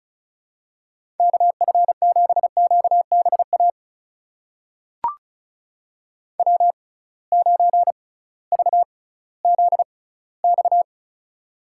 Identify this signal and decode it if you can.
Morse code, heard as a keyed tone, about 36 wpm, spaced as for 19 wpm, 700 Hz. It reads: KF7QBA W9VZX
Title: USA Call Signs - 6x Character Spacing - 35wpm